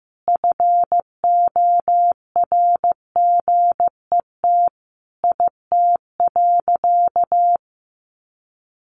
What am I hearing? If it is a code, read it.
Morse code, heard as a keyed tone, about 15 wpm, 700 Hz. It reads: FORGET IT.
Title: Sentences from Top 1200 Words 15wpm